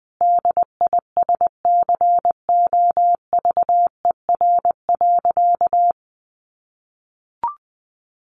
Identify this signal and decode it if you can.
Morse code, heard as a keyed tone, about 20 wpm, 700 Hz. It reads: DISCOVER.
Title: Sentences from Top 1100 Words 20wpm